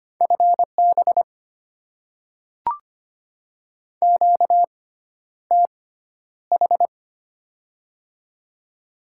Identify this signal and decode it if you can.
Morse code, heard as a keyed tone, about 25 wpm, spaced as for 11 wpm, 700 Hz. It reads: FB QTH
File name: Top 20 QSO Elements - 6x Character Spacing - 25wpm